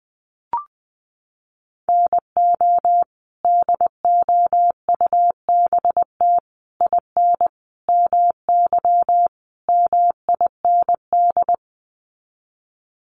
Morse code, heard as a keyed tone, about 20 wpm, 700 Hz. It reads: NO DOUBT IN MY MIND
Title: Sets of 5 Words Encore2 20wpm